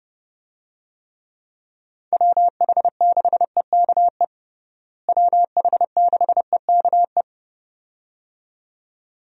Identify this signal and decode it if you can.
Morse code, heard as a keyed tone, about 30 wpm, 700 Hz. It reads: WH6EKE WH6EKE
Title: USA Call Signs - 6 Characters - Repeated 2x - 30wpm